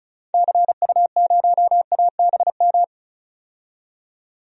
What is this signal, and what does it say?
Morse code, heard as a keyed tone, about 36 wpm, 700 Hz. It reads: CU0ABM
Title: DXCC-100 Call Signs - 6 Characters - 35wpm